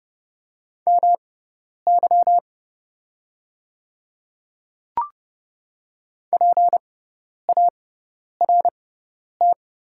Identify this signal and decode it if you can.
Morse code, heard as a keyed tone, about 31 wpm, spaced as for 10 wpm, 700 Hz. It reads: MY PART
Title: 6x Character Spacing - Top 300 Words - 30wpm